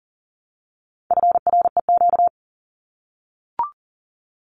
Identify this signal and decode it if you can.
Morse code, heard as a keyed tone, about 40 wpm, 700 Hz. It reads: FREQ
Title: Abbreviations Repeated 3x - 40wpm